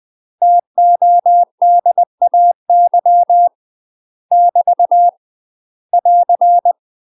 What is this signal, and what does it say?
Morse code, heard as a keyed tone, about 20 wpm, 700 Hz. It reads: TODAY <BT> <AR>
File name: UK MASTER QSOs TRAINING / 20WPM / UK MASTER QSOs 0058